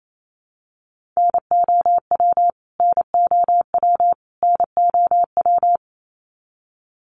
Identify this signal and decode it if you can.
Morse code, heard as a keyed tone, about 28 wpm, 700 Hz. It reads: NOW NOW NOW
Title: Top 300 Words Repeated 28wpm